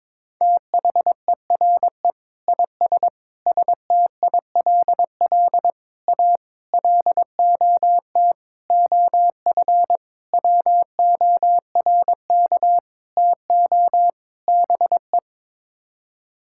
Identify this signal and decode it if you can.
Morse code, heard as a keyed tone, about 22 wpm, 700 Hz. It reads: THERE IS STILL A LOT OF WORK TO BE
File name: Sets of 9 Words - 22wpm